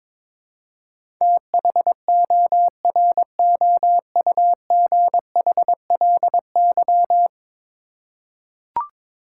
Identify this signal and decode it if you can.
Morse code, heard as a keyed tone, about 22 wpm, 700 Hz. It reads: THOROUGHLY
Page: morse-code-ninja-practice-22wpm